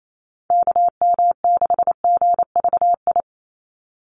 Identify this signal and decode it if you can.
Morse code, heard as a keyed tone, about 28 wpm, 700 Hz. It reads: KM6GVI
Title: USA Call Signs 28wpm